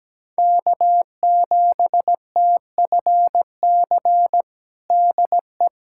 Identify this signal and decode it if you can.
Morse code, heard as a keyed tone, about 17 wpm, 700 Hz. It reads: K7TFC DE
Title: USA Callsign DE Callsign 17wpm